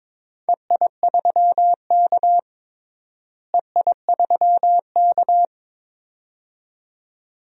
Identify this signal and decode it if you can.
Morse code, heard as a keyed tone, about 22 wpm, 700 Hz. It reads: EI3K EI3K
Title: DXCC-100 Call Signs - 4 Characters - Repeated 2x - 22wpm